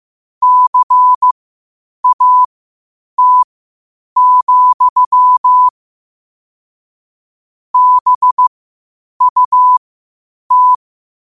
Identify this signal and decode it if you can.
Morse code, heard as a keyed tone, about 15 wpm, spaced as for 8 wpm, 1000 Hz. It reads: CAT, BUT
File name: qotdfiles / 8wpm